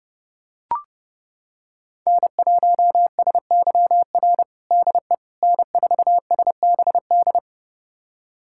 Morse code, heard as a keyed tone, about 32 wpm, 700 Hz. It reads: N1SYR DE N4SBD
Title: USA Callsign DE Callsign - 5 Characters - 30wpm